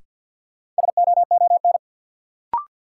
Morse code, heard as a keyed tone, about 48 wpm, 700 Hz. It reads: SOON
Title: Sets of 3 Words - New Word Drill - 50wpm